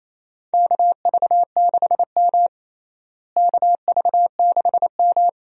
Morse code, heard as a keyed tone, about 28 wpm, 700 Hz. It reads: KV6M KV6M
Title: USA Call Signs - 4 Characters - Repeated 3x - 28wpm